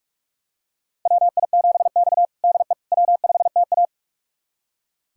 Morse code, heard as a keyed tone, about 42 wpm, 700 Hz. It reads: WI7X DE W5TA
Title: USA Callsign DE Callsign - 4 Characters - 45wpm